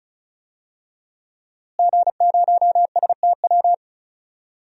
Morse code, heard as a keyed tone, about 34 wpm, 700 Hz. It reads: G0STW
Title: DXCC-100 Call Signs - 5 Characters - Repeated 3x - 35wpm